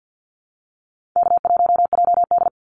Morse code, heard as a keyed tone, about 48 wpm, 700 Hz. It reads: K1JD
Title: NA SOTA Activator Callsigns 50wpm